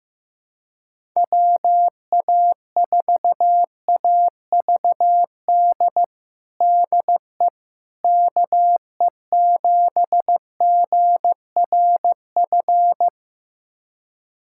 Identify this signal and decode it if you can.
Morse code, heard as a keyed tone, about 15 wpm, 700 Hz. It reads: WA4AVD DE KE7GRF
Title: USA Callsign DE Callsign 15wpm